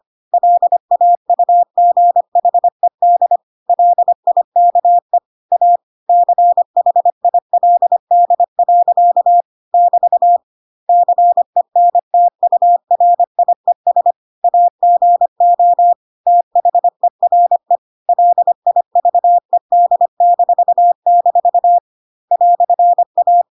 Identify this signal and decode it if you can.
Morse code, heard as a keyed tone, about 25 wpm, 700 Hz. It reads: LAUGHED LIKE A CHILD. <BT> CENTURIES AGO THERE LIVED-- "A
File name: Pinocchio / Pinocchio ch1 25 25 ik1hge